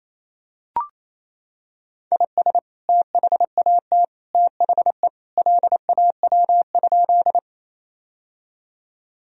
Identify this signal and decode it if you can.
Morse code, heard as a keyed tone, about 27 wpm, 700 Hz. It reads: IS THAT THE LAW?
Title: Sentences from Top 1000 Words - Review Part 3 - 28wpm